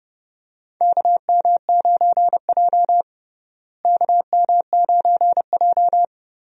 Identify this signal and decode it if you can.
Morse code, heard as a keyed tone, about 32 wpm, 700 Hz. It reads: KM9J KM9J
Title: USA Call Signs - 4 Characters - Repeated 3x - 30wpm